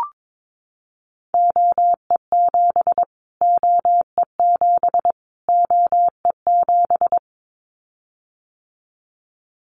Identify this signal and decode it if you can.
Morse code, heard as a keyed tone, about 22 wpm, 700 Hz. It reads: OE7 OE7 OE7
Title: Callsign Prefixes for Top 25 DXCC - Repeated 3x - 22wpm